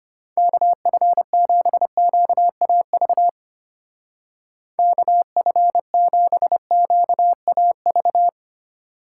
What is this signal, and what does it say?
Morse code, heard as a keyed tone, about 32 wpm, 700 Hz. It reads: KF7QAV KF7QAV